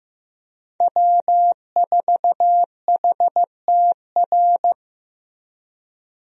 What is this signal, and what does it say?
Morse code, heard as a keyed tone, about 16 wpm, 700 Hz. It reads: W4HTR
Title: USA Call Signs - 5 Characters - Repeated 3x - 15wpm